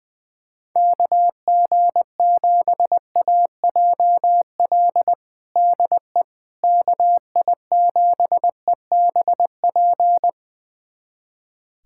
Morse code, heard as a keyed tone, about 21 wpm, 700 Hz. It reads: KG7AJL DE KI7EBP
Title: USA Callsign DE Callsign - 6 Characters - 20wpm